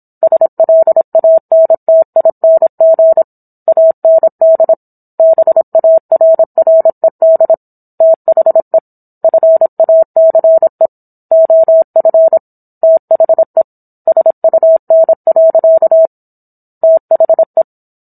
Morse code, heard as a keyed tone, about 27 wpm, 650 Hz. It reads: SLANTING AND BARRED THE FACE OF THE SUN. THE
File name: Book One / 17-Chapter 17 - 26 WPM